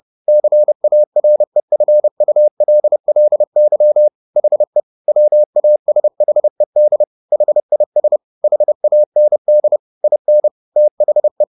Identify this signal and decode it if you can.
Morse code, heard as a keyed tone, about 30 wpm, 600 Hz. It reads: CAREFULLY HE WASHED HIS HAND IN THE
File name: the old man and the sea / the old man and the sea 1 30wpm